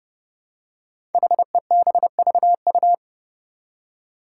Morse code, heard as a keyed tone, about 30 wpm, 700 Hz. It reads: HEBVU
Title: Five Challenging Letter-Number Combinations 30wpm